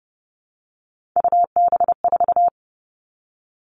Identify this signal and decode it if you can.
Morse code, heard as a keyed tone, about 30 wpm, 700 Hz. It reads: UB4
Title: Three Challenging Letter-Number Combinations 30wpm